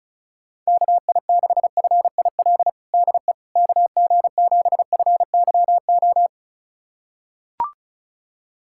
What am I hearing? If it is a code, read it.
Morse code, heard as a keyed tone, about 36 wpm, 700 Hz. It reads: KI6FIL DE KG7FYO
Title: USA Callsign DE Callsign 35wpm